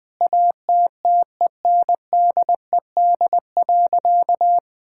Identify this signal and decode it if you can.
Morse code, heard as a keyed tone, about 21 wpm, 700 Hz. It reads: ATTENDED.
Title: Sentences with 4-Word Phrases - 20wpm